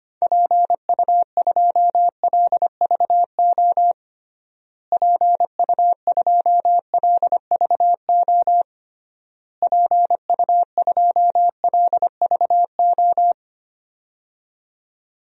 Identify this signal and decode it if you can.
Morse code, heard as a keyed tone, about 25 wpm, 700 Hz. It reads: PU2LVO PU2LVO PU2LVO
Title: DXCC-100 Call Signs - 6 Characters - Repeated 3x - 25wpm